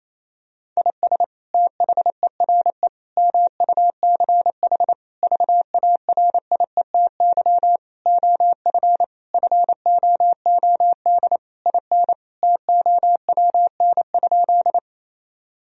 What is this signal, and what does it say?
Morse code, heard as a keyed tone, about 28 wpm, 700 Hz. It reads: IS THERE MUCH VARIETY OF FOOD IN TOWN?
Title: Sentences from Top 1000 Words - Review Part 4 - 28wpm